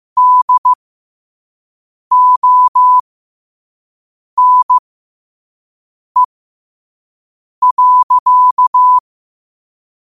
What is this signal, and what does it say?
Morse code, heard as a keyed tone, about 15 wpm, spaced as for 5 wpm, 1000 Hz. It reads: DONE.
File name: qotdfiles / 5wpm